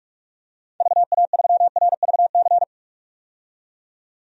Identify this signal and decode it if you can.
Morse code, heard as a keyed tone, about 46 wpm, 700 Hz. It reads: VA3RVC
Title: DXCC-100 Call Signs - 6 Characters - 45wpm